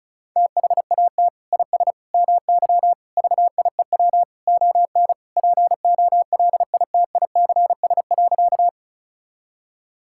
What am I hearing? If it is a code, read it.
Morse code, heard as a keyed tone, about 36 wpm, 700 Hz. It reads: THAT IS MY VIEW ON POLITICS.
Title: Sentences from Top 1000 Words - Review Part 6 - 35wpm